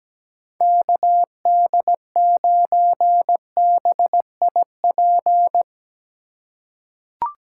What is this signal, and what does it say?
Morse code, heard as a keyed tone, about 17 wpm, 700 Hz. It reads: KD9BIP
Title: USA Call Signs - 6 Characters - 3x Character Spacing - 17wpm